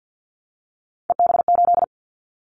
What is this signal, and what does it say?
Morse code, heard as a keyed tone, about 50 wpm, 700 Hz. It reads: EB8
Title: Callsign Prefixes for Top 100 DXCC - 50wpm